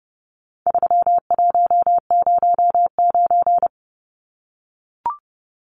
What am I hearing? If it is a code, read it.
Morse code, heard as a keyed tone, about 30 wpm, 700 Hz. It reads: 3109
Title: Numbers - 4-Digit 30wpm